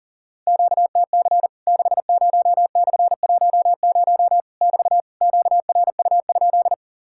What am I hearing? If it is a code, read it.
Morse code, heard as a keyed tone, about 40 wpm, 700 Hz. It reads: QTC 60/10 <BT> QRU?
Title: Worked All Europe DX Contest - QTC Exchange - 40wpm